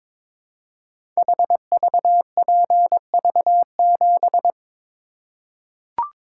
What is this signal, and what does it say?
Morse code, heard as a keyed tone, about 22 wpm, 700 Hz. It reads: HVPV7
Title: Code Groups - 5 Letter:Numbers - 22wpm